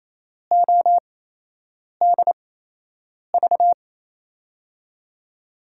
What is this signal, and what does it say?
Morse code, heard as a keyed tone, about 28 wpm, spaced as for 8 wpm, 700 Hz. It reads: ODV
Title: USA Call Signs - 5 Characters - 8x Character Spacing - 28wpm